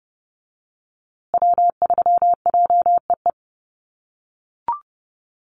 Morse code, heard as a keyed tone, about 28 wpm, 700 Hz. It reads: W3JEE
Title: USA Call Signs Repeated 2x 30wpm